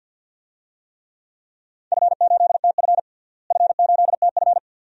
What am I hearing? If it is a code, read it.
Morse code, heard as a keyed tone, about 52 wpm, 700 Hz. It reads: F8TF F8TF
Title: DXCC-100 Call Signs - 4 Characters - Repeated 2x - 50wpm